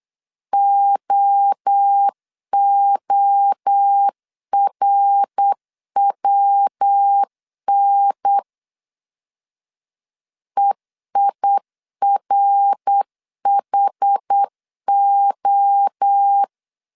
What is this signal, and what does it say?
Morse code, heard as a keyed tone, about 8 wpm, 800 Hz. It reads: OORWN EIRHO
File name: CW / DET / 1 3 1500C S B 8wpm 4141